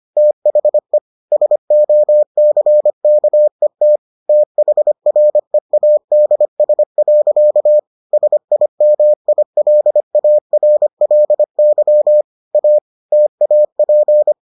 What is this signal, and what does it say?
Morse code, heard as a keyed tone, about 25 wpm, 600 Hz. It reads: THE SOCKET THREADS. SIMILARLY A TAP